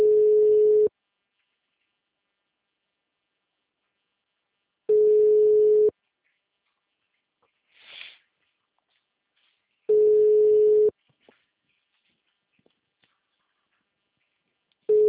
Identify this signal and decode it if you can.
Ringback tone (Europe, 425 Hz).